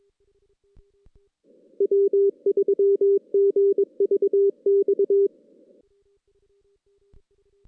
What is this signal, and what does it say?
Morse code, heard as a keyed tone, about 25 wpm, 400 Hz. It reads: W3GVX
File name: 2016 JAN CW NAQP / 2016 Jan naqp k5ot - 0-6